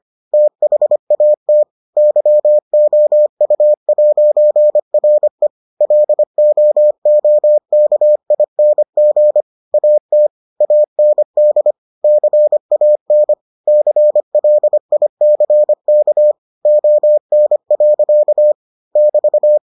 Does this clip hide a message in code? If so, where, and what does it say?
Yes: Morse code from 0.3 to 19.6 s, heard as a keyed tone, about 25 wpm, 600 Hz. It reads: THAT YOU'RE LOOKING AT AND CAN CLICK ON. <BT>